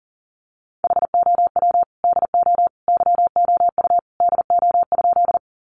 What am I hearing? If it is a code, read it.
Morse code, heard as a keyed tone, about 40 wpm, 700 Hz. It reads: HOW DO YOU DO?
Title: Sentences from Top 100 Words 40wpm